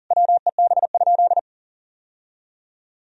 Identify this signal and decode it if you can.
Morse code, heard as a keyed tone, about 40 wpm, 700 Hz. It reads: WEB?
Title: Sentences from Top 1000 Words - Review Part 4 - 40wpm